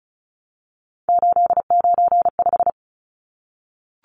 Morse code, heard as a keyed tone, about 34 wpm, 700 Hz. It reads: 895
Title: Numbers - 3-Digit 35wpm